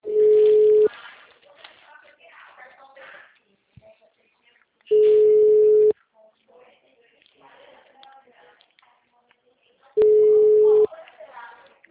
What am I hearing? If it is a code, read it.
Ringback tone (Europe, 425 Hz), with speech.